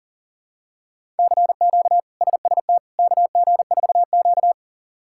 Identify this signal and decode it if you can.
Morse code, heard as a keyed tone, about 40 wpm, 700 Hz. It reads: CQ SST KG4Q
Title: K1USN SST QSO Practice - 40wpm